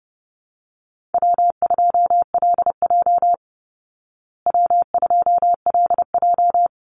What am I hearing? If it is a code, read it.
Morse code, heard as a keyed tone, about 30 wpm, 700 Hz. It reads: W2LJ W2LJ
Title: Easy Call Signs Repeated 2x - 30wpm